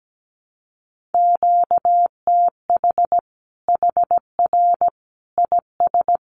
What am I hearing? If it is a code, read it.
Morse code, heard as a keyed tone, about 17 wpm, 700 Hz. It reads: QTH HR IS
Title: Common QSO Phrases 17wpm